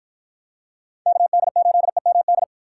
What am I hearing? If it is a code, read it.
Morse code, heard as a keyed tone, about 47 wpm, 700 Hz. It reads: KD8EMD